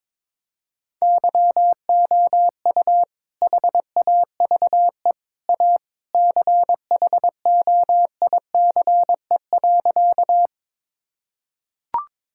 Morse code, heard as a keyed tone, about 21 wpm, 700 Hz. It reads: YOU HAVE A CHOICE.
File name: practice / 22wpm / Sentences with 3-Word Phrases - 22wpm